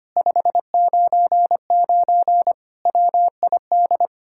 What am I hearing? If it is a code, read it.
Morse code, heard as a keyed tone, about 33 wpm, 700 Hz. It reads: 599 WID